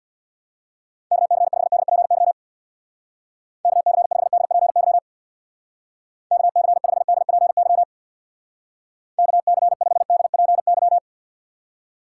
Morse code, heard as a keyed tone, about 63 wpm, 700 Hz. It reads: KC5DPY KC5DPY KC5DPY KC5DPY